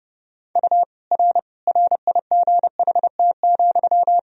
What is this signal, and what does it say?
Morse code, heard as a keyed tone, about 30 wpm, 700 Hz. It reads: U R RIGHT,
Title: Sentences from Top 400 Words and Abbreviations 30wpm